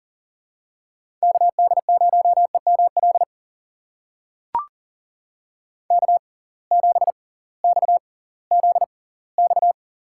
Morse code, heard as a keyed tone, about 40 wpm, spaced as for 22 wpm, 700 Hz. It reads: KD0EML K7XZX
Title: USA Call Signs - 6x Character Spacing - 40wpm